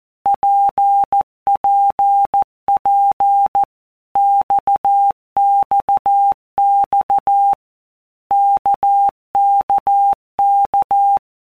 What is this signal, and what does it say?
Morse code, heard as a keyed tone, about 14 wpm, 800 Hz. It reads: PPP XXX KKK
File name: All Groups 1-minute Blocks / Groups@13 / RKPX